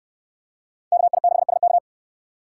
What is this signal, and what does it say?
Morse code, heard as a keyed tone, about 67 wpm, 700 Hz. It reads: KI6SK